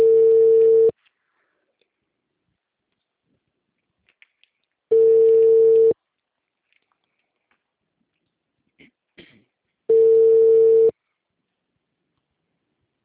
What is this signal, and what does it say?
Ringback tone (China, 450 Hz).